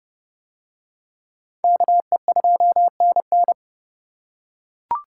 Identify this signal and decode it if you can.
Morse code, heard as a keyed tone, about 28 wpm, 700 Hz. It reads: KE2NN